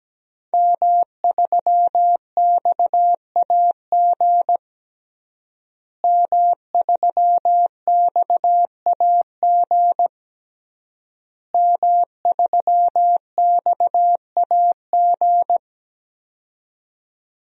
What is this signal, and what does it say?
Morse code, heard as a keyed tone, about 17 wpm, 700 Hz. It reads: M3XAG M3XAG M3XAG
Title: DXCC-100 Call Signs - 5 Characters - Repeated 3x - 17wpm